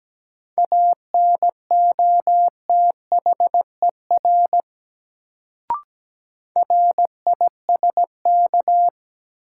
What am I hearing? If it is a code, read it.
Morse code, heard as a keyed tone, about 17 wpm, 700 Hz. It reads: ANOTHER RISK
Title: Sentences from Top 600 Words - New Word Drill - 17wpm